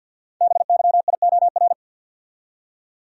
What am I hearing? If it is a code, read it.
Morse code, heard as a keyed tone, about 48 wpm, 700 Hz. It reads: BYIOR
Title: Code Groups - 5 Letters - 50wpm